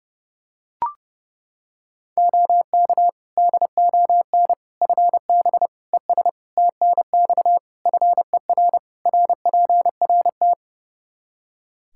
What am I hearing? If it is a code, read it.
Morse code, heard as a keyed tone, about 30 wpm, 700 Hz. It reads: OK DON FB ES TNX FER RPRT
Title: Practice QSO - Exchange 2 - Info - 30wpm